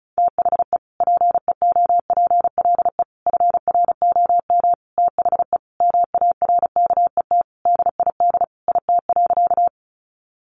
Morse code, heard as a keyed tone, about 34 wpm, 700 Hz. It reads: THE PEOPLE FROM THE MARKET DID IT.
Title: Sentences from Top 600 Words 35wpm